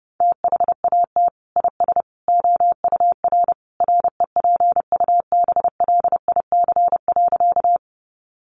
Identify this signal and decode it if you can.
Morse code, heard as a keyed tone, about 28 wpm, 700 Hz. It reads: THAT IS OUR REPUBLIC.
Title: Sentences from Top 1500 Words 30wpm